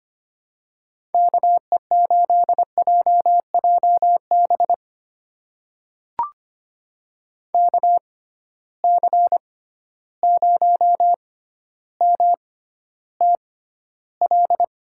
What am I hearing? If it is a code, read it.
Morse code, heard as a keyed tone, about 25 wpm, spaced as for 13 wpm, 700 Hz. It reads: KE8JJB KC0MTL
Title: USA Call Signs - 6 Characters - 6x Character Spacing - 25wpm